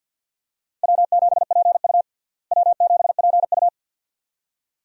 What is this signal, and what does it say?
Morse code, heard as a keyed tone, about 51 wpm, 700 Hz. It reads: W7PU W7PU
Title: USA Call Signs - 4 Characters - Repeated 2x - 50wpm